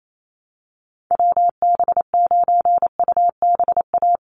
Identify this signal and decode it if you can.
Morse code, heard as a keyed tone, about 28 wpm, 700 Hz. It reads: WB9UBA
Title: USA Call Signs 28wpm